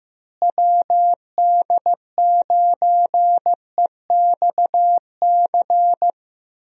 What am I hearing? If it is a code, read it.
Morse code, heard as a keyed tone, about 15 wpm, 700 Hz. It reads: WD9EXC
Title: USA Call Signs - 6 Characters - Repeated 3x - 15wpm